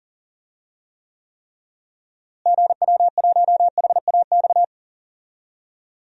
Morse code, heard as a keyed tone, about 39 wpm, 700 Hz. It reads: GW1HAX
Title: DXCC-100 Call Signs - 6 Characters - Repeated 3x - 40wpm